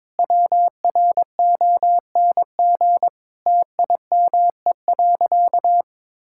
Morse code, heard as a keyed tone, about 22 wpm, 700 Hz. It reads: WRONG TIME.
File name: Sentences with 9-Word Phrases - 22wpm